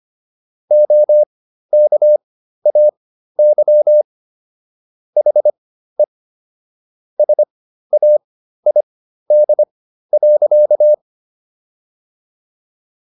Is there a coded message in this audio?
Yes: Morse code from 0.7 to 10.9 s, heard as a keyed tone, about 25 wpm, spaced as for 13 wpm, 600 Hz. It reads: OKAY HE SAID.